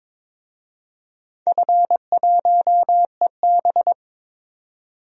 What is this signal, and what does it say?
Morse code, heard as a keyed tone, about 22 wpm, 700 Hz. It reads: F1EB
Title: DXCC-100 Call Signs - 4 Characters - 22wpm